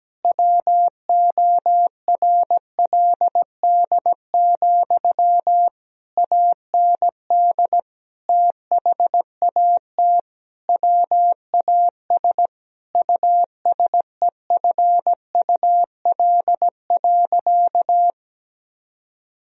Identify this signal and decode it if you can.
Morse code, heard as a keyed tone, about 17 wpm, 700 Hz. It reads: WORLD, AND THAT WAS USEFUL.
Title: Sentences from Top 1000 Words - Review Part 4 - 17wpm